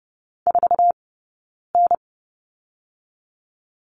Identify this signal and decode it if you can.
Morse code, heard as a keyed tone, about 30 wpm, 700 Hz. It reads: 4 N
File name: Lesson 15d - Number 5 - Two Characters 30wpm